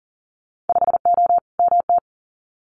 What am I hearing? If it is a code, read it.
Morse code, heard as a keyed tone, about 42 wpm, 700 Hz. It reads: 5O MT
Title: Winter Field Day Exchanges Encore 40wpm